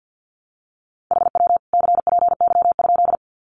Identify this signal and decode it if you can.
Morse code, heard as a keyed tone, about 48 wpm, 700 Hz. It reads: HW CPY?
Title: Common QSO Phrases 50wpm